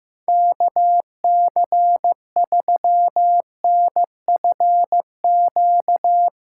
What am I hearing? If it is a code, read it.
Morse code, heard as a keyed tone, about 16 wpm, 700 Hz. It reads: KC3NFQ
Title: USA Callsign DE Callsign - 6 Characters - 15wpm